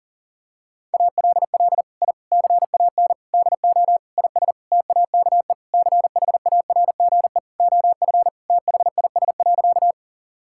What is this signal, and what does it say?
Morse code, heard as a keyed tone, about 37 wpm, 700 Hz. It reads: ALL I CAN DO IS TAKE CHARGE OF THIS.
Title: Sentences from Top 1000 Words - Review Part 6 - 40wpm